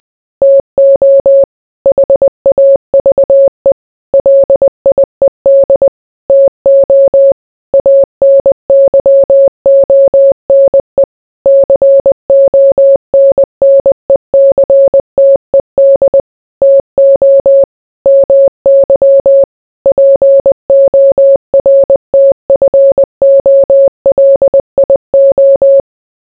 Morse code, heard as a keyed tone, about 20 wpm, 550 Hz. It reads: TO HAVE LIED TO ANYONE CONNECTED TO MY PORTFOLIO